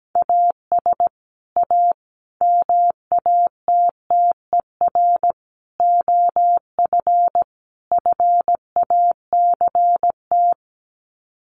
Morse code, heard as a keyed tone, about 17 wpm, 700 Hz. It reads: AS A MATTER OF FACT
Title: Sets of 5 Words 17wpm